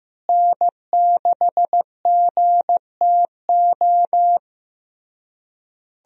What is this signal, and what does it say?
Morse code, heard as a keyed tone, about 15 wpm, 700 Hz. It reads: N6GTO
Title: USA Call Signs - 5 Characters - Repeated 3x - 15wpm